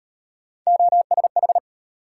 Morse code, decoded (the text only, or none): OSH